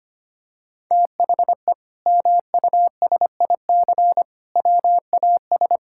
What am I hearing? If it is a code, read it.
Morse code, heard as a keyed tone, about 25 wpm, 700 Hz. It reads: THE MUSIC WAS
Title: Sentences from Top 1000 Words - Review Part 6 - 25wpm